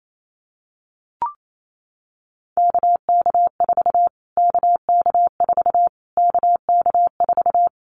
Morse code, heard as a keyed tone, about 28 wpm, 700 Hz. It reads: KK4 KK4 KK4
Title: Callsign Prefixes for Top 25 DXCC - Repeated 3x - 28wpm